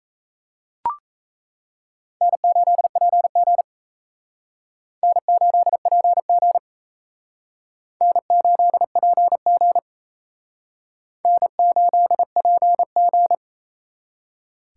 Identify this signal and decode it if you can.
Morse code, heard as a keyed tone, about 34 wpm, 700 Hz. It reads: N8PG N8PG N8PG N8PG
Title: Speed Racing - USA 1x2 Call Signs 28wpm